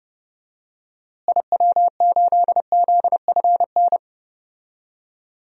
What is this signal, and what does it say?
Morse code, heard as a keyed tone, about 30 wpm, 700 Hz. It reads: IW8ZFN
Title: DXCC-100 Call Signs - 6 Characters - Repeated 3x - 30wpm